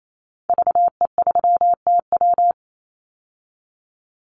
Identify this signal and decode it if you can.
Morse code, heard as a keyed tone, about 28 wpm, 700 Hz. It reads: VE3TW
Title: Call Signs - 28wpm